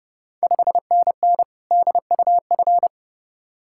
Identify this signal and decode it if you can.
Morse code, heard as a keyed tone, about 31 wpm, 700 Hz. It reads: 5NN DUF
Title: Ontario QSO Party Exchange 30wpm